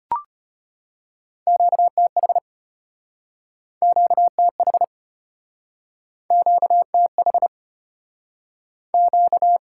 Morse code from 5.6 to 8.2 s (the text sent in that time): QTH